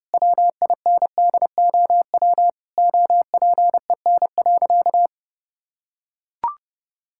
Morse code, heard as a keyed tone, about 32 wpm, 700 Hz. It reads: WINDOW OPEN.
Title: Sentences from Top 1600 Words 30wpm